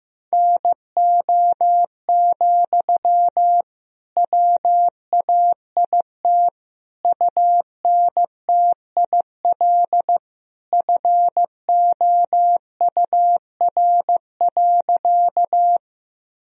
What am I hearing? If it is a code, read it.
Morse code, heard as a keyed tone, about 15 wpm, 700 Hz. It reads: NO, WAIT UNTIL FOUR.
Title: Sentences from Top 1000 Words - Review Part 1 - 15wpm